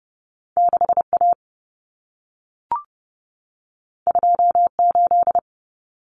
Morse code, heard as a keyed tone, about 30 wpm, 700 Hz. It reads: 6A 28
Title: Common Cut Numbers - 2-Digit 30wpm